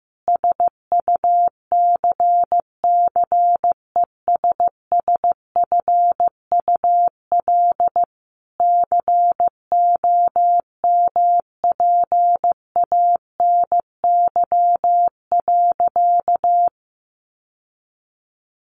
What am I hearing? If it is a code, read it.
Morse code, heard as a keyed tone, about 14 wpm, 700 Hz. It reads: SUCCESSFUL COMPANY.
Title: Sentences from Top 1100 Words 15wpm